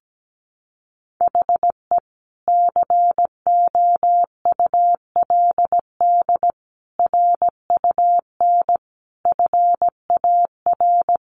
Morse code, heard as a keyed tone, about 17 wpm, 700 Hz. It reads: HE COULD RUN FAR
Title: Sentences from Top 1100 Words 17wpm